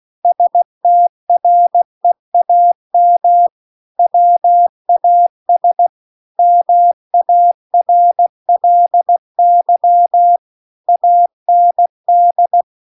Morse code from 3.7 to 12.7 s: WAS MARLY AND